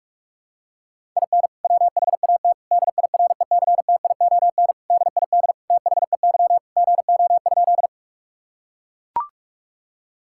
Morse code, heard as a keyed tone, about 41 wpm, 700 Hz. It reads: IN WHAT DIRECTION DID THEY GO?
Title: Sentences from Top 1000 Words - Review Part 5 - 45wpm